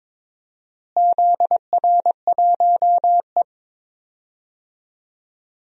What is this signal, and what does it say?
Morse code, heard as a keyed tone, about 21 wpm, 700 Hz. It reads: ZR1E
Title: DXCC-100 Call Signs - 4 Characters - Repeated 3x - 22wpm